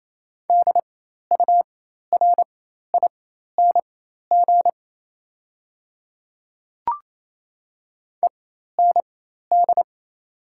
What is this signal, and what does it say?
Morse code, heard as a keyed tone, about 28 wpm, spaced as for 13 wpm, 700 Hz. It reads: DURING END
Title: 4x Character Spacing - Top 500 Words - 28wpm